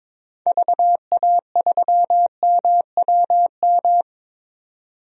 Morse code, heard as a keyed tone, about 21 wpm, 700 Hz. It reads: VA3MWM
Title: DXCC-100 Call Signs - 6 Characters - 22wpm